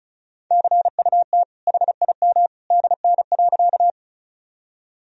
Morse code, heard as a keyed tone, about 36 wpm, 700 Hz. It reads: CUT HIM DN.
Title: Sentences from Top 300 Words and Abbreviations 35wpm